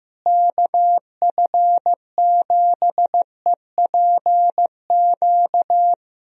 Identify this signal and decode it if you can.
Morse code, heard as a keyed tone, about 15 wpm, 700 Hz. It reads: KF7EPQ
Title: USA Call Signs - 6 Characters - Repeated 2x - 15wpm